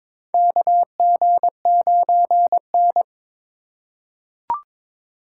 Morse code, heard as a keyed tone, about 22 wpm, 700 Hz. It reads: KG9N
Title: USA Call Signs - 4 Characters - 3x Character Spacing - 22wpm